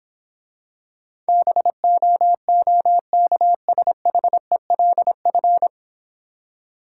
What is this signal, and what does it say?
Morse code, heard as a keyed tone, about 26 wpm, 690 Hz. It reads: BOOKSHELF